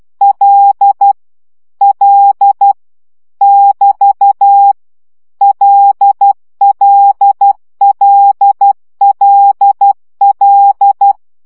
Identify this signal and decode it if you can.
Morse code, heard as a keyed tone, about 12 wpm, 800 Hz. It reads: L L <BT> LLLLL
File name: CW gyakorlas l 10 baud